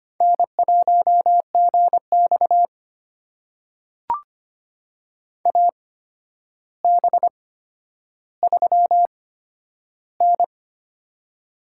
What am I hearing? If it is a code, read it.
Morse code, heard as a keyed tone, about 25 wpm, spaced as for 11 wpm, 700 Hz. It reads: N1GX AB3N
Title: USA Call Signs - 4 Characters - 8x Character Spacing - 25wpm